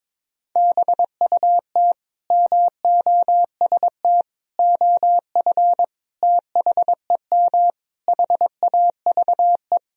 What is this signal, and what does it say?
Morse code, heard as a keyed tone, about 22 wpm, 700 Hz. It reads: BUT MOST OF THEM HAVE
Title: Sentences from Top 1000 Words - Review Part 3 - 22wpm